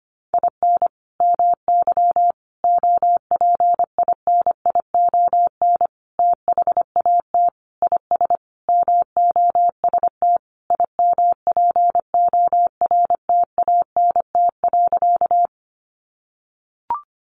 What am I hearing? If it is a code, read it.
Morse code, heard as a keyed tone, about 25 wpm, 700 Hz. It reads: IN MY OPINION THAT IS MOST IMPORTANT.
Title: Sentences from Top 1200 Words 25wpm